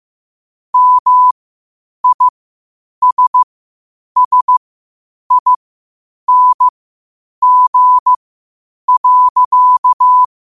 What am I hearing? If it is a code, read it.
Morse code, heard as a keyed tone, about 15 wpm, spaced as for 8 wpm, 1000 Hz. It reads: MISSING.
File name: qotdfiles / 8wpm